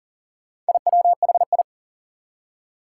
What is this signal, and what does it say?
Morse code, heard as a keyed tone, about 39 wpm, 700 Hz. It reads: IWHI
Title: Four Challenging Letter-Number Combinations 40wpm